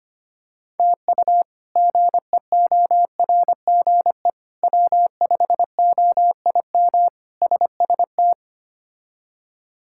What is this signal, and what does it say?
Morse code, heard as a keyed tone, about 25 wpm, 700 Hz. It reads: TU GEORGE W5OIM SST
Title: K1USN SST QSO Practice - 25wpm